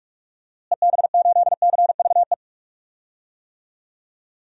Morse code, heard as a keyed tone, about 42 wpm, 700 Hz. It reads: EB8CVE
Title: DXCC-100 Call Signs - 6 Characters - Repeated 3x - 45wpm